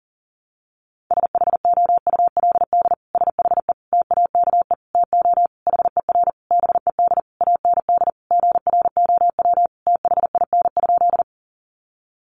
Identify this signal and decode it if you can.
Morse code, heard as a keyed tone, about 40 wpm, 700 Hz. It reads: SHOULD SHE TAKE TO HER BED AND GROW THIN?
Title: Sentences from Top 1200 Words 40wpm